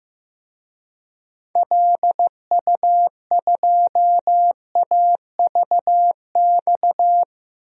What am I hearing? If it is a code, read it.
Morse code, heard as a keyed tone, about 15 wpm, 700 Hz. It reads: LU2AVX
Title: DXCC-100 Call Signs - 6 Characters - 15wpm